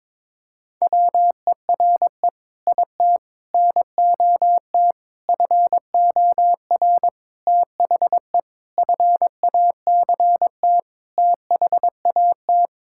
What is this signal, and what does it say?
Morse code, heard as a keyed tone, about 22 wpm, 700 Hz. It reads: WERE IT NOT FOR THE FACT THAT
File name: practice / 22wpm / Sets of 7 Words - 22wpm